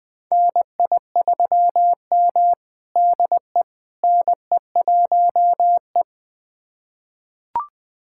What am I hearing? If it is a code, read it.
Morse code, heard as a keyed tone, about 20 wpm, 700 Hz. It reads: NI3M DE NE1E
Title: USA Callsign DE Callsign - 4 Characters - 20wpm